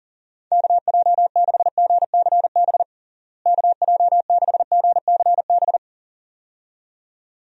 Morse code, heard as a keyed tone, about 37 wpm, 700 Hz. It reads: KJ6GCB KJ6GCB
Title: USA Call Signs - 6 Characters - Repeated 2x - 40wpm